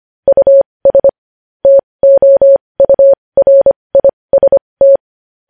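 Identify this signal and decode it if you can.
Morse code, heard as a keyed tone, about 25 wpm, spaced as for 20 wpm, 550 Hz. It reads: US TOURIST
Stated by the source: Morse code news headlines sent at 20 wpm.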